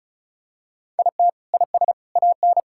Morse code, heard as a keyed tone, about 34 wpm, 700 Hz. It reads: IT IS AN
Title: Sentences from Top 1000 Words - Review Part 5 - 35wpm